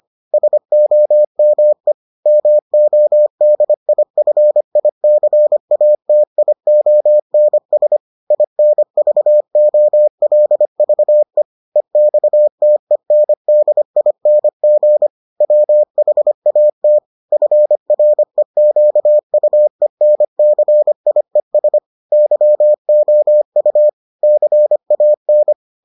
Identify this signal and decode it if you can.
Morse code, heard as a keyed tone, about 25 wpm, 600 Hz. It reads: SOME MODIFICATIONS INVOLVE EXTENDING WHAT FREQUENCIES YOU CAN